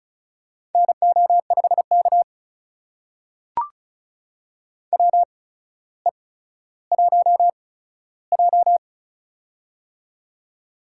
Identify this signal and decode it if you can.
Morse code, heard as a keyed tone, about 36 wpm, spaced as for 15 wpm, 700 Hz. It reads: NO5K WE1J
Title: USA Call Signs - 4 Characters - 8x Character Spacing - 35wpm